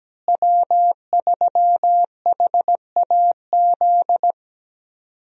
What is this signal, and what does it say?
Morse code, heard as a keyed tone, about 17 wpm, 700 Hz. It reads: W3HAZ
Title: USA Call Signs - 5 Characters - Repeated 3x - 17wpm